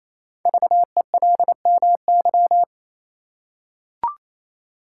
Morse code, heard as a keyed tone, about 28 wpm, 700 Hz. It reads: VELMY
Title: Code Groups - 5 Letters - 28wpm